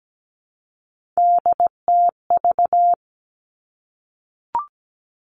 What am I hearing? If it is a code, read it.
Morse code, heard as a keyed tone, about 17 wpm, 700 Hz. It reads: DTV
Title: Three Challenging Letter-Number Combinations 17wpm